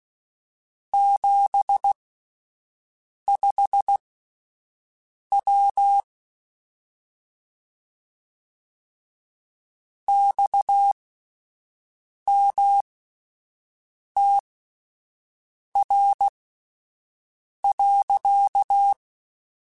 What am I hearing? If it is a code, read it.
Morse code, heard as a keyed tone, about 16 wpm, spaced as for 5 wpm, 800 Hz. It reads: 75W XMTR.
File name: (spaced) / QSOs Spaced-mp3s / WA7QOX 1 @10 sp